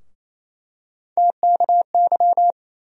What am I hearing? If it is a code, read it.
Morse code, heard as a keyed tone, about 28 wpm, 700 Hz. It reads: TKY
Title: USA Call Sign Postfixes - 3 Letters 28wpm